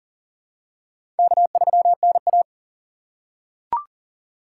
Morse code, heard as a keyed tone, about 40 wpm, 700 Hz. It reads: K3NA
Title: USA Call Signs - 4 Characters - 3x Character Spacing - 40wpm